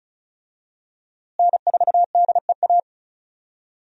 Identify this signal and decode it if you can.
Morse code, heard as a keyed tone, about 34 wpm, 700 Hz. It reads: N4DEA
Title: USA Call Signs - 5 Characters - Repeated 3x - 35wpm